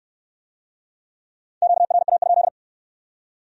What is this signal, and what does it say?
Morse code, heard as a keyed tone, about 62 wpm, 700 Hz. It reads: QRA?